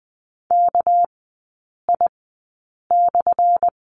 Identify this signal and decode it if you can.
Morse code, heard as a keyed tone, about 20 wpm, spaced as for 8 wpm, 700 Hz. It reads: KI/
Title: 198 Lesson 32e - Letter k - Three Characters 20wpm